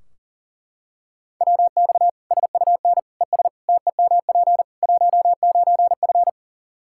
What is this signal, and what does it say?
Morse code, heard as a keyed tone, about 40 wpm, 700 Hz. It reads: WX SUN ES TEMP 19F
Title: Practice QSO - Exchange 2 - Info - 40wpm